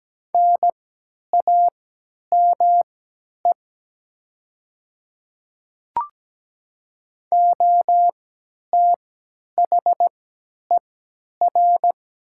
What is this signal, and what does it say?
Morse code, heard as a keyed tone, about 17 wpm, spaced as for 10 wpm, 700 Hz. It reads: NAME OTHER
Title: 3x Character Spacing - Top 300 Words - 17wpm